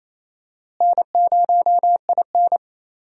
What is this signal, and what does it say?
Morse code, heard as a keyed tone, about 28 wpm, 700 Hz. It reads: N0IN
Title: USA Call Signs - 4 Characters - Repeated 2x - 28wpm